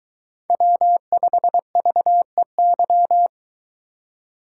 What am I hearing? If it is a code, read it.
Morse code, heard as a keyed tone, about 23 wpm, 700 Hz. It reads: W5VEY